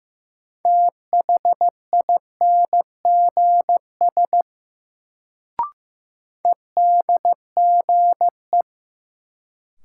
Morse code, heard as a keyed tone, about 15 wpm, 700 Hz. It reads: THINGS EDGE
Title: Sets of 3 Words - New Word Drill - 15wpm